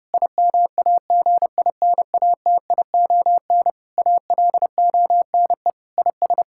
Morse code, heard as a keyed tone, about 30 wpm, 700 Hz. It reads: IMAGINATION ALONE IS
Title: Sentences from Top 1600 Words 30wpm